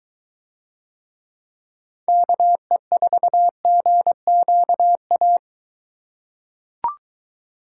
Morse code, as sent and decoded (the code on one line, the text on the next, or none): -.- . ....- --. --.- .-
KE4GQA